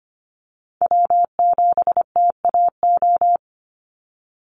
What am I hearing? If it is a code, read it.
Morse code, heard as a keyed tone, about 25 wpm, 700 Hz. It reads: W7TAO
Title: NA SOTA Chaser Callsigns 25wpm